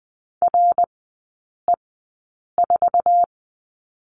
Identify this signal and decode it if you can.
Morse code, heard as a keyed tone, about 20 wpm, spaced as for 8 wpm, 700 Hz. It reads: RE4
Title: 053 Lesson 10e - Letter r - Three Characters 20wpm